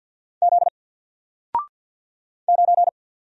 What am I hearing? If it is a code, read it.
Morse code, heard as a keyed tone, about 48 wpm, 700 Hz. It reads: Z 9
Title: Single Character (Letter:Number:Punctuation:Prosign) 50wpm